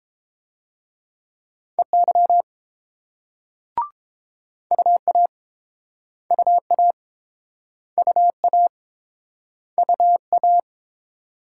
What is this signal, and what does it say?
Morse code, heard as a keyed tone, about 31 wpm, 700 Hz. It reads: EY UA UA UA UA